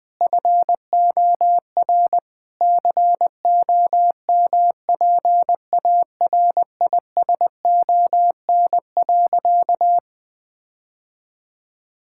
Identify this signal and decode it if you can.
Morse code, heard as a keyed tone, about 21 wpm, 700 Hz. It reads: FOR COMPARISON.
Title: Sentences from Top 1600 Words 20wpm